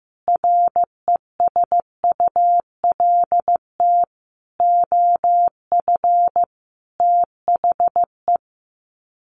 Morse code, heard as a keyed tone, about 15 wpm, 700 Hz. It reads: RESULT OF THE
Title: Sets of 5 Words 15wpm